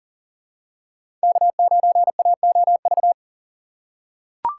Morse code, heard as a keyed tone, about 39 wpm, 700 Hz. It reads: K9AOV
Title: USA Call Signs - 5 Characters - 4x Character Spacing - 40wpm